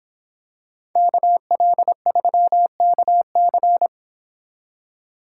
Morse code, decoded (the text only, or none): KL3KC